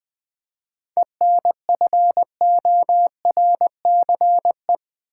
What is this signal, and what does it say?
Morse code, heard as a keyed tone, about 20 wpm, 700 Hz. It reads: ENFORCE
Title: 7 Letter Words - First 3 Letters Given - 20wpm